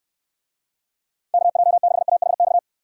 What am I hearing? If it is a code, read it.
Morse code, heard as a keyed tone, about 69 wpm, 700 Hz. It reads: KJ6AHX